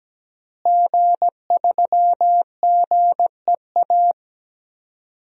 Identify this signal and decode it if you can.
Morse code, heard as a keyed tone, about 17 wpm, 700 Hz. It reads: G3GEA
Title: DXCC-100 Call Signs - 5 Characters - Repeated 3x - 17wpm